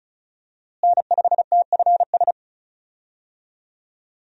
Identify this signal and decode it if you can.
Morse code, heard as a keyed tone, about 36 wpm, 700 Hz. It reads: N5TFS
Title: USA Call Signs - 5 Characters - Repeated 2x - 35wpm